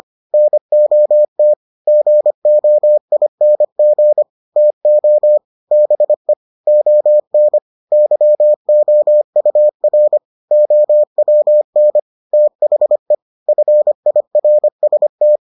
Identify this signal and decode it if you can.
Morse code, heard as a keyed tone, about 25 wpm, 600 Hz. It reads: NOT GOING TO BE ON YOUR OWN THE FIRST